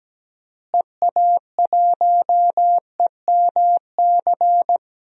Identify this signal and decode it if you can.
Morse code, heard as a keyed tone, about 17 wpm, 700 Hz. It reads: EA1EMC
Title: DXCC-100 Call Signs - 6 Characters - Repeated 3x - 17wpm